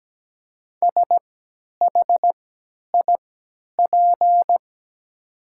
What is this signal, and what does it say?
Morse code, heard as a keyed tone, about 17 wpm, spaced as for 10 wpm, 700 Hz. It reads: SHIP
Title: 3x Character Spacing - Top 500 Words - 17wpm